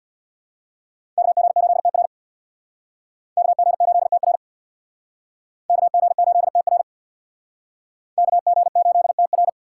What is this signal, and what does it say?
Morse code, heard as a keyed tone, about 62 wpm, 700 Hz. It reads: KG8TR KG8TR KG8TR KG8TR
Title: Speed Racing - USA 2x2 Call Signs 50wpm